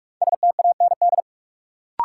Morse code, heard as a keyed tone, about 44 wpm, 700 Hz. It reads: STAND
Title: Sentences from Top 300 Words - New Word Drill - 45wpm